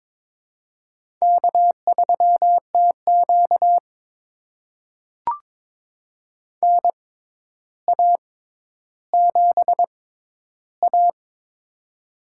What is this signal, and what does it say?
Morse code, heard as a keyed tone, about 23 wpm, spaced as for 11 wpm, 700 Hz. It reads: K3TQ NA7A
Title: USA Call Signs - 4 Characters - 6x Character Spacing - 22wpm